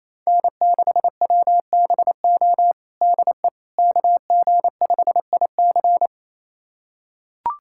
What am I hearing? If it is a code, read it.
Morse code, heard as a keyed tone, about 28 wpm, 700 Hz. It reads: N6WBO DE KG5IC
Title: USA Callsign DE Callsign - 5 Characters - 28wpm